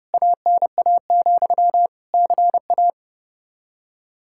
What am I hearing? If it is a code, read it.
Morse code, heard as a keyed tone, about 32 wpm, 700 Hz. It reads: ANA, CA
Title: Practice QSO - Exchange 1 - Beginning - 30wpm